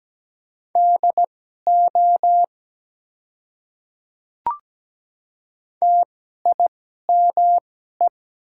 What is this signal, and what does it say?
Morse code, heard as a keyed tone, about 17 wpm, spaced as for 12 wpm, 700 Hz. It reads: DO TIME
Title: 2x Character Spacing - Top 100 Words - 17wpm